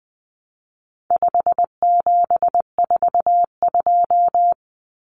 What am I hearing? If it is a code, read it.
Morse code, heard as a keyed tone, about 20 wpm, 700 Hz. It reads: 5742
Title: Common Cut Numbers - 4-Digit 20wpm